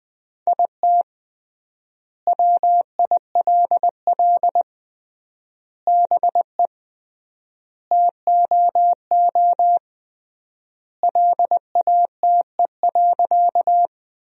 Morse code, heard as a keyed tone, about 20 wpm, 700 Hz. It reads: IT WILL BE TOO LATE.
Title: 3x Word-Spacing - Sentences from Top 300 Words - 20wpm